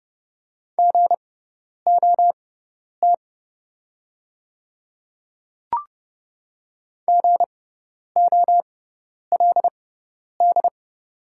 Morse code, heard as a keyed tone, about 29 wpm, spaced as for 10 wpm, 700 Hz. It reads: GOT GOLD
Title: 6x Character Spacing - Top 500 Words - 30wpm